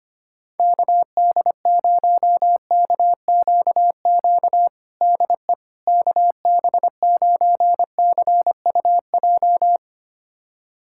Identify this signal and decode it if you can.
Morse code, heard as a keyed tone, about 25 wpm, 700 Hz. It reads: KD0KQQ DE KB9CUJ
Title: USA Callsign DE Callsign - 6 Characters - 25wpm